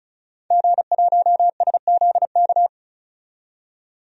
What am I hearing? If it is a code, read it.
Morse code, heard as a keyed tone, about 34 wpm, 700 Hz. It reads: G1SZK
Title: DXCC-100 Call Signs - 5 Characters - 35wpm